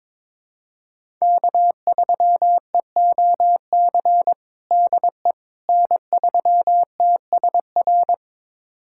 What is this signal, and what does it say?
Morse code, heard as a keyed tone, about 22 wpm, 700 Hz. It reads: K3EOC DE N3TSR
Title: USA Callsign DE Callsign - 5 Characters - 22wpm